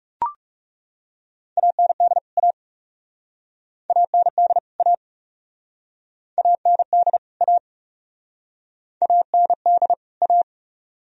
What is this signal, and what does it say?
Morse code, heard as a keyed tone, about 41 wpm, 700 Hz. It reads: AND A AND A AND A AND A